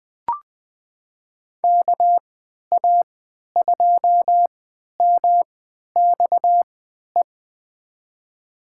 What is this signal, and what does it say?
Morse code, heard as a keyed tone, about 21 wpm, spaced as for 12 wpm, 700 Hz. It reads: KA2MXE
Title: USA Call Signs - 6 Characters - 3x Character Spacing - 20wpm